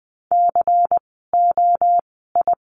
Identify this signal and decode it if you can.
Morse code, heard as a keyed tone, about 20 wpm, spaced as for 15 wpm, 700 Hz. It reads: COI
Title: 094 Lesson 16f - Letter c - Words 20wpm